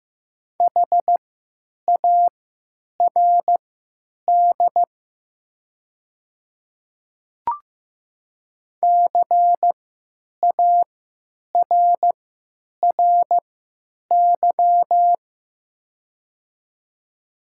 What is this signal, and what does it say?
Morse code, heard as a keyed tone, about 15 wpm, spaced as for 9 wpm, 700 Hz. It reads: HARD CARRY
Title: 3x Character Spacing - Top 300 Words - 15wpm